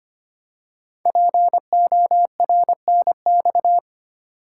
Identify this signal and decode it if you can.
Morse code, heard as a keyed tone, about 25 wpm, 700 Hz. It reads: PORNX
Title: Code Groups - 5 Letters - 25wpm